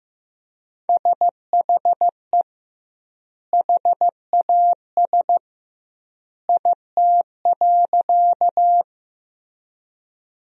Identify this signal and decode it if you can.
Morse code, heard as a keyed tone, about 15 wpm, 700 Hz. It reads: SHE HAS IT.
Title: 2x Word-Spacing - Sentences from Top 100 Words - 15wpm